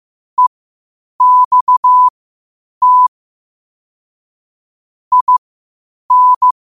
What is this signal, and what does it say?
Morse code, heard as a keyed tone, about 15 wpm, spaced as for 8 wpm, 1000 Hz. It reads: EXT IN
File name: qotdfiles / 8wpm